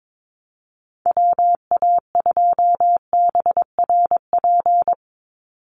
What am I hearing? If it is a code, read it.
Morse code, heard as a keyed tone, about 22 wpm, 700 Hz. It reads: WA2BRP
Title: USA Call Signs 22wpm